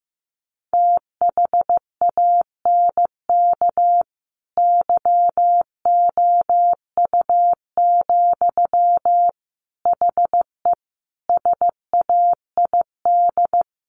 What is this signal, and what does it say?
Morse code, heard as a keyed tone, about 15 wpm, 700 Hz. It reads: THANK YOU, HE SAID
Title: Sentences from Top 1500 Words 15wpm